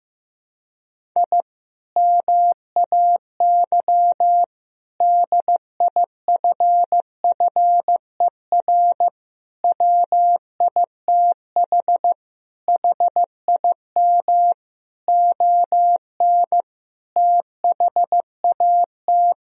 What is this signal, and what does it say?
Morse code, heard as a keyed tone, about 16 wpm, 700 Hz. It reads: I MAY DIFFER WITH HIM ON THAT
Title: Sentences from Top 1000 Words - Review Part 1 - 15wpm